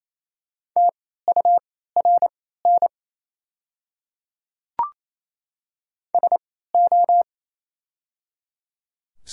Morse code, heard as a keyed tone, about 28 wpm, spaced as for 16 wpm, 700 Hz. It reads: TURN SO
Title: 3x Character Spacing - Top 200 Words - 28wpm